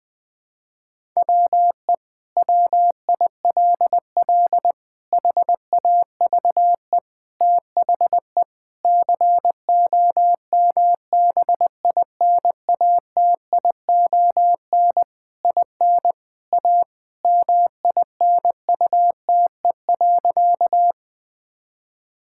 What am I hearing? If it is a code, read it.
Morse code, heard as a keyed tone, about 20 wpm, 700 Hz. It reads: WE WILL HAVE THE COMBINATION IN A MINUTE.
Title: Sentences from Top 1000 Words - Review Part 6 - 20wpm